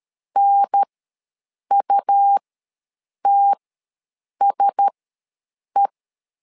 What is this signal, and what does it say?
Morse code, heard as a keyed tone, about 13 wpm, spaced as for 7 wpm, 800 Hz. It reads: NUTSE